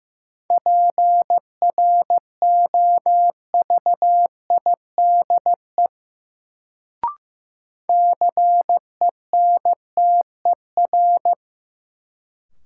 Morse code, heard as a keyed tone, about 15 wpm, 700 Hz. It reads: PROVIDE CENTER
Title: Sentences from Top 1000 Words - Review Part 2 - New Word Drill - 15wpm